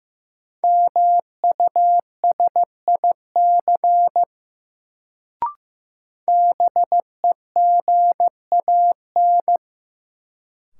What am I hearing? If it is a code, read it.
Morse code, heard as a keyed tone, about 15 wpm, 700 Hz. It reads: MUSIC BEGAN
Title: Sentences from Top 300 Words - New Word Drill - 15wpm